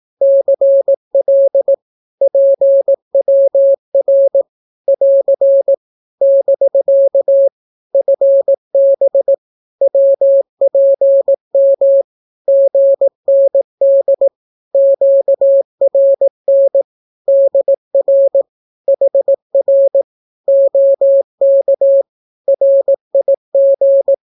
Morse code, heard as a keyed tone, about 18 wpm, 550 Hz. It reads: CL PWR <AR> <BK> FB WPM GND QRN DR HR OK RIG
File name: naukapodstaw / skroty / skroty